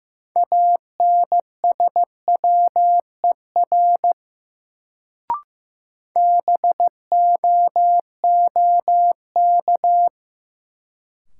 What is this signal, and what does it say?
Morse code, heard as a keyed tone, about 15 wpm, 700 Hz. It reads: ANSWER BOOK
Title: Sentences from Top 300 Words - New Word Drill - 15wpm